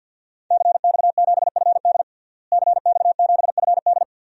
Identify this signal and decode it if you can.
Morse code, heard as a keyed tone, about 51 wpm, 700 Hz. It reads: CX7FD CX7FD
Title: DXCC-100 Call Signs - 5 Characters - Repeated 2x - 50wpm